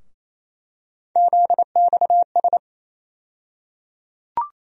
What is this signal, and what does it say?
Morse code, heard as a keyed tone, about 27 wpm, 700 Hz. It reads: ZXS
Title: USA Call Sign Postfixes - 3 Letters 28wpm